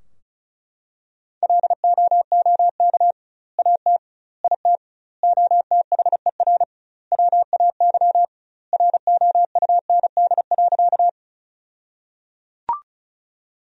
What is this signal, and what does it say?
Morse code, heard as a keyed tone, about 36 wpm, 700 Hz. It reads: LOOK AT IT OTHER WAY ROUND.
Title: 2x Word-Spacing - Sentences from Top 200 Words - 35wpm